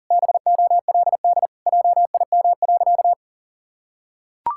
Morse code, heard as a keyed tone, about 37 wpm, 700 Hz. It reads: BOLD JIM.
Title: Pangram Challenge v2 - 40wpm